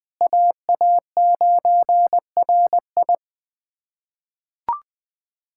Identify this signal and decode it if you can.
Morse code, heard as a keyed tone, about 20 wpm, 700 Hz. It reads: AA9RI
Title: USA Callsign DE Callsign - 5 Characters - 20wpm